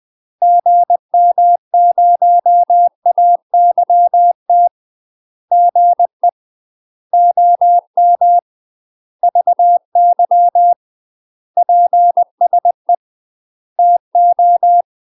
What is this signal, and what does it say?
Morse code, heard as a keyed tone, about 20 wpm, 700 Hz. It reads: GM0AYT GE OM VY PSE TO
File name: UK MASTER QSOs TRAINING / 20WPM / UK MASTER QSOs 0003